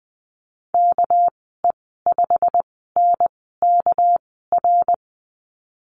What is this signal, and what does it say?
Morse code, heard as a keyed tone, about 20 wpm, spaced as for 15 wpm, 700 Hz. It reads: KE5NKR
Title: 207 Lesson 33g - Letter j - Call Signs 20wpm